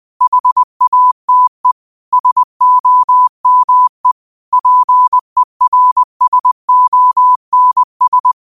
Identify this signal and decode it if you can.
Morse code, heard as a keyed tone, about 20 wpm, 1000 Hz. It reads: HATE SOME PERSONS
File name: qotdfiles / 20wpm